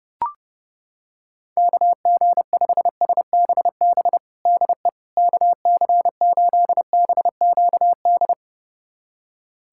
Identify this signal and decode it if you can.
Morse code, heard as a keyed tone, about 30 wpm, 690 Hz. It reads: KG5SBB DE KC8BQD
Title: USA Callsign DE Callsign - 6 Characters - 30wpm